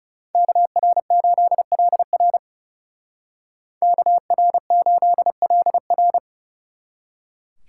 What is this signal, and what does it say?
Morse code, heard as a keyed tone, about 35 wpm, 700 Hz. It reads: KR8LR KR8LR
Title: Speed Racing - USA 2x2 Call Signs 30wpm